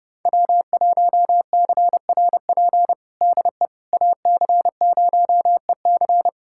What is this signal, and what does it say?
Morse code, heard as a keyed tone, about 30 wpm, 700 Hz. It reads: W1CRP DE AC0EC
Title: USA Callsign DE Callsign - 5 Characters - 30wpm